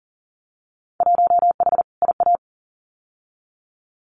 Morse code, heard as a keyed tone, about 40 wpm, 700 Hz. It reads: 1H IA
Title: Winter Field Day Exchanges 40wpm